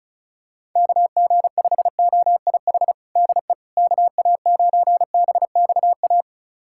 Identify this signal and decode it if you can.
Morse code, heard as a keyed tone, about 34 wpm, 690 Hz. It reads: KG5OIH DE KA9BXA
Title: USA Callsign DE Callsign - 6 Characters - 35wpm